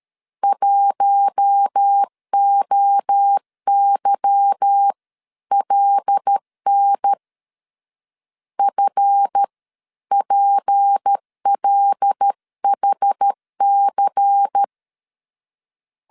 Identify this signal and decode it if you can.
Morse code, heard as a keyed tone, about 13 wpm, spaced as for 11 wpm, 800 Hz. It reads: 1OYLN FPLHC
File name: LEZIONI / 8 YJ45 / 8 Y 12wpm 2228